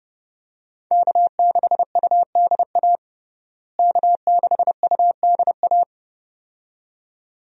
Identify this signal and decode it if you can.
Morse code, heard as a keyed tone, about 32 wpm, 700 Hz. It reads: K6UDA K6UDA
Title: USA Call Signs - 5 Characters - Repeated 2x - 30wpm